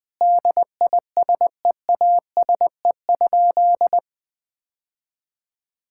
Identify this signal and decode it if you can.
Morse code, heard as a keyed tone, about 19 wpm, 700 Hz. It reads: DISEASE?
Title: Sentences from Top 1000 Words - Review Part 6 - 20wpm